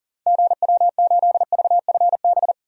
Morse code, heard as a keyed tone, about 40 wpm, 700 Hz. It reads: GW8VFB
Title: DXCC-100 Call Signs - 6 Characters - 40wpm